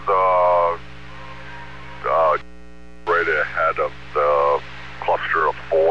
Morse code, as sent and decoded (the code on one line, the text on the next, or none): none